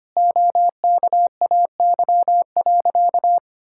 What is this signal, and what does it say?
Morse code, heard as a keyed tone, about 25 wpm, 700 Hz. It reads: OKAY.
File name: practice / 25wpm / Sentences with 2-Word Phrases - 25wpm